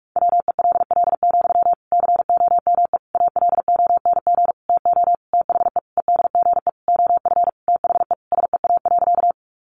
Morse code, heard as a keyed tone, about 44 wpm, 700 Hz. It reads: WELL, COME ALONG TO THE EDGE OF THE SEA.
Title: Sentences from Top 1300 Words 45wpm